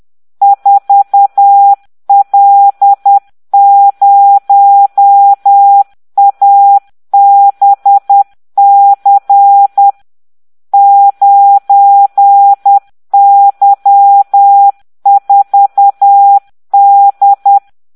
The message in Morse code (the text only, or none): 4L0ABC 9Y4D